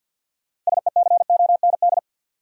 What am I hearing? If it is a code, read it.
Morse code, heard as a keyed tone, about 53 wpm, 700 Hz. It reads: SECOND
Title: Sentences from Top 300 Words - New Word Drill - 50wpm